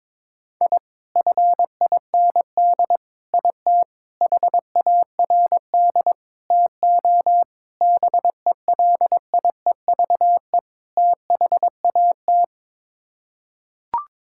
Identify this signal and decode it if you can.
Morse code, heard as a keyed tone, about 22 wpm, 700 Hz. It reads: I FIND IT HARD TO BELIEVE THAT
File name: practice / 22wpm / Sets of 7 Words - 22wpm